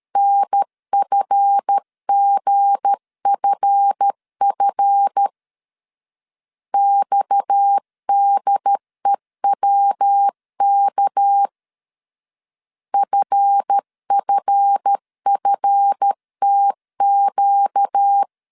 Morse code, heard as a keyed tone, about 13 wpm, 800 Hz. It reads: NFGFF XDEWK FFFTQ
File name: LEZIONI / 7 GF9Z / 7 GF 12wpm 2122